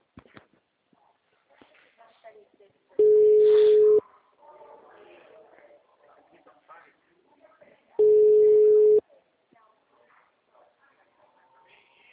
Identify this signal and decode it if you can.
Ringback tone (Europe, 425 Hz), with speech.